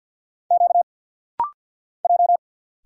Morse code, heard as a keyed tone, about 50 wpm, 690 Hz. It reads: Q J
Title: Single Letter-Number - Rapid-Fire 50wpm